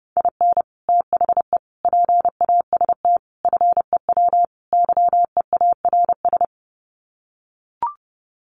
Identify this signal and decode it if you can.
Morse code, heard as a keyed tone, about 30 wpm, 700 Hz. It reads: IN THE PAST FEW YEARS
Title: Sets of 5 Words 30wpm